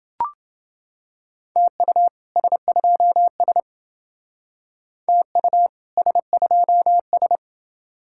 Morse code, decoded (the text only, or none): TU S2S TU S2S